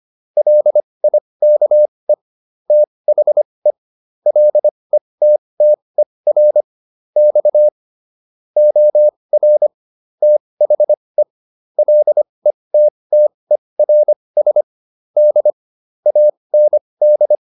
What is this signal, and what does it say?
Morse code, heard as a keyed tone, about 25 wpm, spaced as for 20 wpm, 600 Hz. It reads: LIKE THE LETTER X OR THE LETTERS D AND